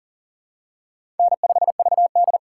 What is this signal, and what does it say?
Morse code, heard as a keyed tone, about 40 wpm, 700 Hz. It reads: N5VD
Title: USA Call Signs - 4 Characters - 4x Character Spacing - 40wpm